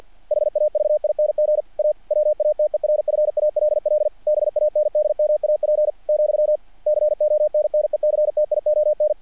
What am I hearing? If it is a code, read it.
Morse code, heard as a keyed tone, about 48 wpm, 600 Hz, with radio noise. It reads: HAVING A WATERFALL BANDMAP , CONNECTION
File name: archived / CRSnet